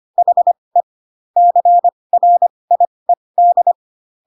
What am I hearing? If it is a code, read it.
Morse code, heard as a keyed tone, about 25 wpm, spaced as for 20 wpm, 700 Hz. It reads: HE CRIED
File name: Pinocchio / Pinocchio ch3 15 25 ik1hge